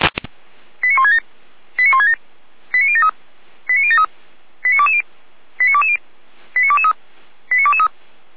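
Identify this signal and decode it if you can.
Five-tone selective call (ZVEI), heard as quick runs of five tones, tones of about 70 ms. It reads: 89167 89167 89073 89073 89220 89220 89223 89223
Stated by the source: ZVEI five-tone selective call received by SDR.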